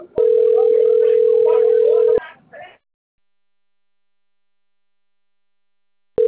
Ringback tone (North America), 440 + 480 Hz.
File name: rg-1600-unknown- - - .69386